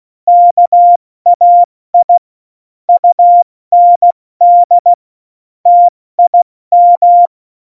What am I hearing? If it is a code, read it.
Morse code, heard as a keyed tone, about 16 wpm, 700 Hz. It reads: KAI UND TIM
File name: Morsekurs / Witze / Witz 07. November 2017, 70 F80